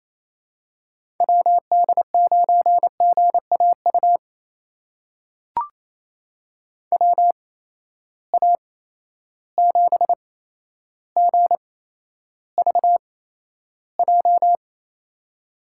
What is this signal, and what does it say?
Morse code, heard as a keyed tone, about 27 wpm, spaced as for 12 wpm, 700 Hz. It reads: WD9GAU WA7GVJ
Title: USA Call Signs - 6 Characters - 8x Character Spacing - 28wpm